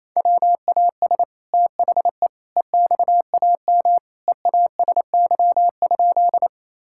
Morse code, heard as a keyed tone, about 28 wpm, 700 Hz. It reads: WAS THE EXAM EASY?
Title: Sentences from Top 1000 Words - Review Part 5 - 28wpm